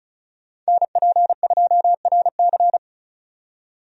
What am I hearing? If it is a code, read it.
Morse code, heard as a keyed tone, about 34 wpm, 700 Hz. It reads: NP2RC
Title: USA Call Signs - 5 Characters - 35wpm